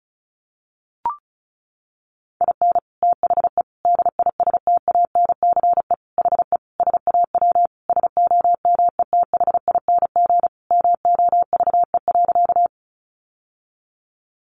Morse code, heard as a keyed tone, about 34 wpm, 700 Hz. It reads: IN THE DISTANCE HE SAW SOMETHING MOVE.
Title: Sentences from Top 1200 Words 35wpm